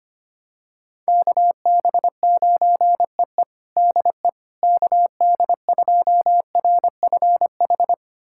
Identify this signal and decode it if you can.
Morse code, heard as a keyed tone, about 25 wpm, 700 Hz. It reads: KB9EE DE KD2RFH
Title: USA Callsign DE Callsign 25wpm